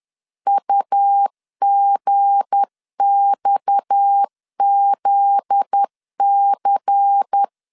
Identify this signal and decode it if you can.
Morse code, heard as a keyed tone, about 10 wpm, 800 Hz. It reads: UGXZC